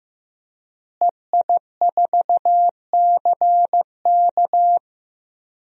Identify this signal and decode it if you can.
Morse code, heard as a keyed tone, about 15 wpm, 700 Hz. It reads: EI4CK
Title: DXCC-100 Call Signs - 5 Characters - Repeated 3x - 15wpm